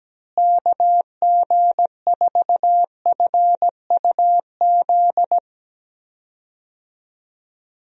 Morse code, heard as a keyed tone, about 17 wpm, 700 Hz. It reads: KG4FUZ
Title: USA Call Signs - 6 Characters - Repeated 2x - 17wpm